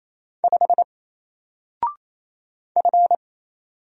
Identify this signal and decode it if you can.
Morse code, heard as a keyed tone, about 27 wpm, 700 Hz. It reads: 5 F
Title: Single Character (Letter:Number:Punctuation:Prosign) 28wpm